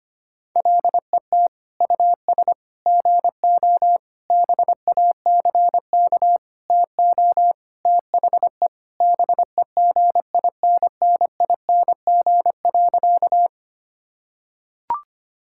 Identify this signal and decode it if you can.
Morse code, heard as a keyed tone, about 25 wpm, 700 Hz. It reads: LET US GO BACK TO THE BEGINNING.
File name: practice / 25wpm / Sentences with 4-Word Phrases - 25wpm